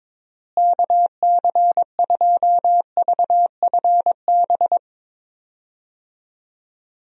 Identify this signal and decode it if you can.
Morse code, heard as a keyed tone, about 22 wpm, 700 Hz. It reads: KC2VFB
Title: USA Call Signs - 6 Characters - Repeated 2x - 22wpm